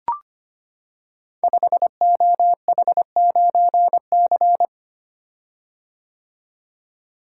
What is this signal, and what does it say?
Morse code, heard as a keyed tone, about 25 wpm, 700 Hz. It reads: 5OH9C
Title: Code Groups - 5 Letter:Numbers - 25wpm